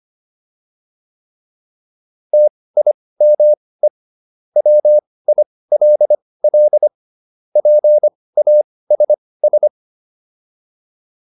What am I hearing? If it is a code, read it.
Morse code, heard as a keyed tone, about 25 wpm, spaced as for 18 wpm, 600 Hz. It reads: TIME WILL PASS
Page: intermediate-practice-files